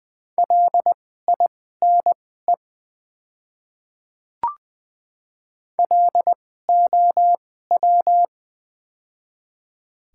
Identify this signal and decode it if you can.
Morse code, heard as a keyed tone, about 21 wpm, spaced as for 15 wpm, 700 Hz. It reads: LINE LOW
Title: 2x Character Spacing - Top 500 Words - 20wpm